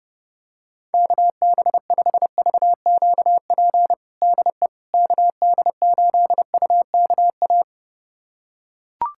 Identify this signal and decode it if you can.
Morse code, heard as a keyed tone, about 32 wpm, 700 Hz. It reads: KB5VQP DE KD8UKA
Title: USA Callsign DE Callsign - 6 Characters - 30wpm